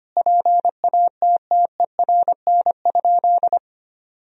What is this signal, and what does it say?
Morse code, heard as a keyed tone, about 25 wpm, 700 Hz. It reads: PATTERN?
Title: Sentences from Top 1000 Words - Review Part 2 - 25wpm